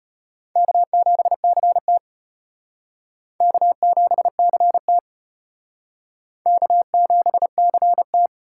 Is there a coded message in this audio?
Yes: Morse code from 0.6 to 8.3 s, heard as a keyed tone, about 33 wpm, 700 Hz. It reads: K7CT K7CT K7CT